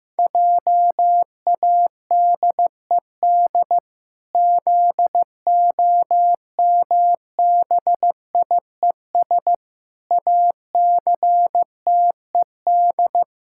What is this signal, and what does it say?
Morse code, heard as a keyed tone, about 15 wpm, 700 Hz. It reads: JADED ZOMBIES ACTED
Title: Pangram Challenge v2 - 15wpm